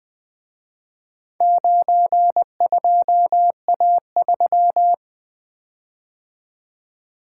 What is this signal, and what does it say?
Morse code, heard as a keyed tone, about 24 wpm, 700 Hz. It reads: 92A3
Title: Speed-Racing - Four Letter-Number Combinations - 20wpm